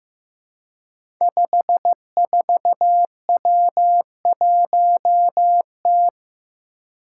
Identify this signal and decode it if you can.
Morse code, heard as a keyed tone, about 15 wpm, 700 Hz. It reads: 54W1T
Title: Five Challenging Letter-Number Combinations 15wpm